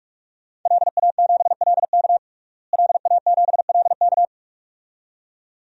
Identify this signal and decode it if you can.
Morse code, heard as a keyed tone, about 41 wpm, 700 Hz. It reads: LA7LK LA7LK
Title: DXCC-100 Call Signs - 5 Characters - Repeated 2x - 45wpm